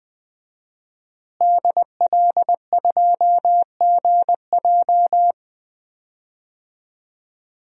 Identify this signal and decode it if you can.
Morse code, heard as a keyed tone, about 20 wpm, 700 Hz. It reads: DL2GJ
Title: DXCC-100 Call Signs - 5 Characters - Repeated 2x - 20wpm